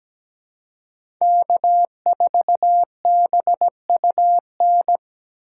Morse code, heard as a keyed tone, about 17 wpm, 700 Hz. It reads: K4BUN
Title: USA Call Signs - 8x Character Spacing - 17wpm